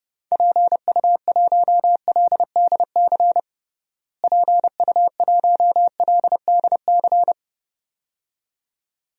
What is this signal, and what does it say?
Morse code, heard as a keyed tone, about 30 wpm, 700 Hz. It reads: PU1LDC PU1LDC
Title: DXCC-100 Call Signs - 6 Characters - Repeated 2x - 30wpm